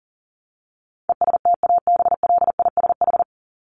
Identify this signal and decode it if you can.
Morse code, heard as a keyed tone, about 40 wpm, 700 Hz. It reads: ESTABLISH
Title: ED Suffix - Top 100 Words with Base Words 40wpm